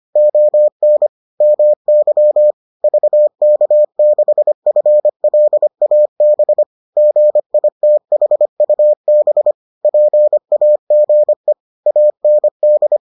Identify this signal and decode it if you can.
Morse code, heard as a keyed tone, about 25 wpm, 600 Hz. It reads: ON MY VK6FLAB GITHUB PAGE AND